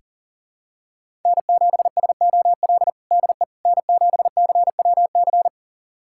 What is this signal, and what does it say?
Morse code, heard as a keyed tone, about 40 wpm, 700 Hz. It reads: N7SOL DE N7CWC
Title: USA Callsign DE Callsign - 5 Characters - 40wpm